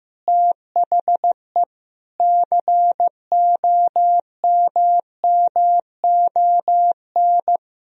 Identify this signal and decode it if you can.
Morse code, heard as a keyed tone, about 15 wpm, 700 Hz. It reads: THE COMMON
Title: Sentences from Top 1000 Words - Review Part 3 - 15wpm